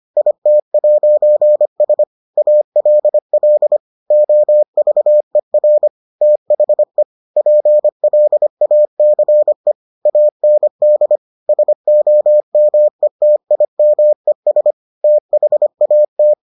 Morse code, heard as a keyed tone, about 25 wpm, 600 Hz. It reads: IT'S ALL OVER THE PLACE AND SOMETIMES THAT